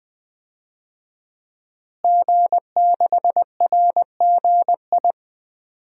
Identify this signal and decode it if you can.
Morse code, heard as a keyed tone, about 20 wpm, 700 Hz. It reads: G6RGI
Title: DXCC-100 Call Signs - 5 Characters - Repeated 2x - 20wpm